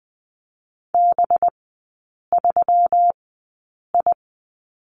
Morse code, heard as a keyed tone, about 20 wpm, spaced as for 8 wpm, 700 Hz. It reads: B3I
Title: 198 Lesson 32e - Letter k - Three Characters 20wpm